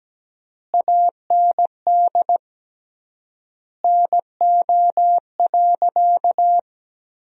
Morse code, heard as a keyed tone, about 17 wpm, 700 Hz. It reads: AND NO.
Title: 3x Word-Spacing - Sentences from Top 500 Words - 17wpm